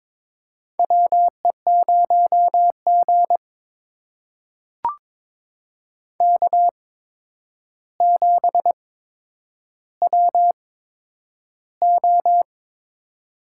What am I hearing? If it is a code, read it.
Morse code, heard as a keyed tone, about 22 wpm, spaced as for 9 wpm, 700 Hz. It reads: WE0G K7WO
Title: USA Call Signs - 4 Characters - 8x Character Spacing - 22wpm